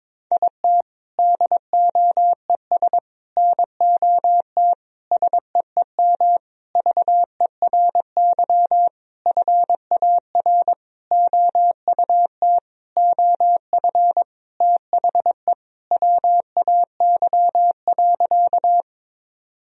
Morse code, heard as a keyed tone, about 21 wpm, 700 Hz. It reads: IT DOES NOT SEEM VERY FAR OUT OF THE WAY.
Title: Sentences from Top 1000 Words - Review Part 1 - 22wpm